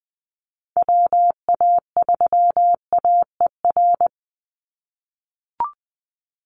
Morse code, heard as a keyed tone, about 20 wpm, 700 Hz. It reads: WA3AER
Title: Call Signs - 20wpm